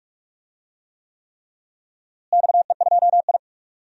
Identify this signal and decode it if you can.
Morse code, heard as a keyed tone, about 46 wpm, 700 Hz. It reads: XE2I
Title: DXCC-100 Call Signs - 4 Characters - Repeated 3x - 45wpm